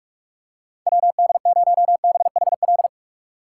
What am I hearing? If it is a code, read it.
Morse code, heard as a keyed tone, about 46 wpm, 700 Hz. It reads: WD0BHL
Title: USA Call Signs - 6 Characters - 4x Character Spacing - 45wpm